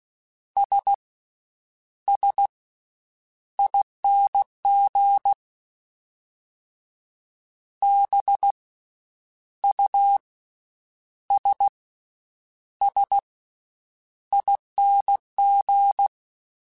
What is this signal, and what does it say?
Morse code, heard as a keyed tone, about 16 wpm, 800 Hz. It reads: S S ING B U S S ING
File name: Special INGs-mp3s / Special INGs-mp3s / Slow ing @15